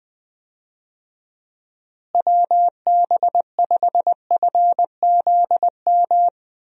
Morse code, heard as a keyed tone, about 20 wpm, 700 Hz. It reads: WB5FZM
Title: USA Call Signs - 6 Characters - Repeated 3x - 20wpm